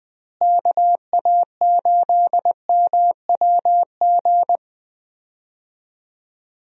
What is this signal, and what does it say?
Morse code, heard as a keyed tone, about 20 wpm, 700 Hz. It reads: KA8MWG
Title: USA Call Signs - 6 Characters - Repeated 2x - 20wpm